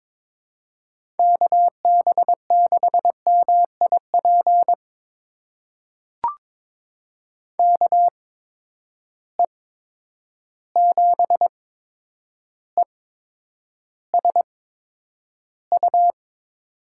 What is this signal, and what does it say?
Morse code, heard as a keyed tone, about 22 wpm, spaced as for 9 wpm, 700 Hz. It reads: KB6MIP KE7ESU
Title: USA Call Signs - 6 Characters - 8x Character Spacing - 22wpm